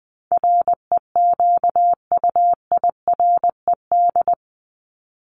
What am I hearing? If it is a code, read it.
Morse code, heard as a keyed tone, about 20 wpm, 700 Hz. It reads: REQUIRED
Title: ED Suffix - Top 100 Words with Base Words 20wpm